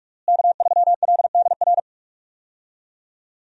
Morse code, heard as a keyed tone, about 43 wpm, 700 Hz. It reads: K3LDR
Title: USA Call Signs - 5 Characters - Repeated 3x - 45wpm